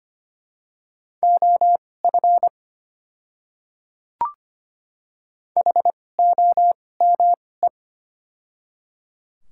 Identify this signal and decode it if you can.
Morse code, heard as a keyed tone, about 25 wpm, spaced as for 18 wpm, 700 Hz. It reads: OF HOME
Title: 2x Character Spacing - Top 300 Words - 25wpm